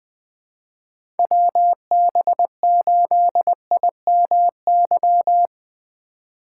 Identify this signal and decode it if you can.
Morse code, heard as a keyed tone, about 20 wpm, 700 Hz. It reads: WB8IMY
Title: K1USN SST QSO Practice - 20wpm